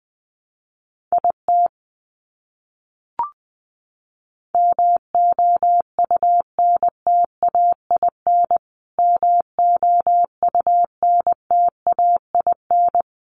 Morse code, heard as a keyed tone, about 20 wpm, 700 Hz. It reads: IT MOUNTAIN MOUNTAIN
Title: Top 500 Words Repeated 20wpm